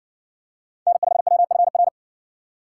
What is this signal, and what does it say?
Morse code, heard as a keyed tone, about 60 wpm, 690 Hz. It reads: N5WFR